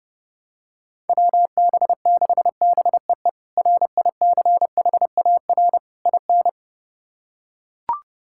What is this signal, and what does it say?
Morse code, heard as a keyed tone, about 30 wpm, 700 Hz. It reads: WB6BEE RICHAR IN
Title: K1USN SST QSO Practice - 30wpm